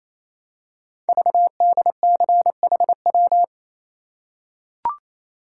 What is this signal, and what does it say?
Morse code, heard as a keyed tone, about 29 wpm, 700 Hz. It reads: VDCHW
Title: Code Groups - 5 Letters - 28wpm